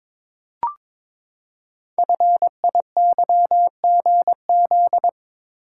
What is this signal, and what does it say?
Morse code, heard as a keyed tone, about 22 wpm, 700 Hz. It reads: FIYGZ
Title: Code Groups - 5 Letters - 22wpm